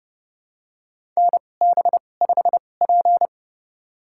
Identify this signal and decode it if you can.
Morse code, heard as a keyed tone, about 30 wpm, spaced as for 22 wpm, 700 Hz. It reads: NB5P
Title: USA Call Signs - 4 Characters - 2x Character Spacing - 30wpm